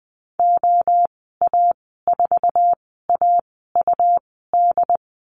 Morse code, heard as a keyed tone, about 20 wpm, spaced as for 15 wpm, 700 Hz. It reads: OA4AUD
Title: 147 Lesson 24g - Letter y - Call Signs 20wpm